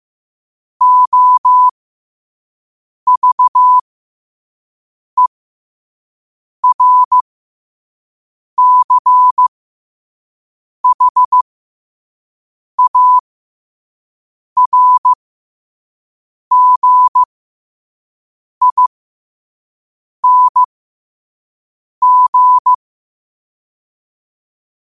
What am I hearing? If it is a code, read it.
Morse code, heard as a keyed tone, about 15 wpm, spaced as for 5 wpm, 1000 Hz. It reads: OVERCHARGING
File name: qotdfiles / 5wpm